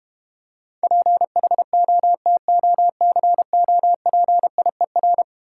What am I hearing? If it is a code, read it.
Morse code, heard as a keyed tone, about 32 wpm, 700 Hz. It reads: PHOTOCOPIER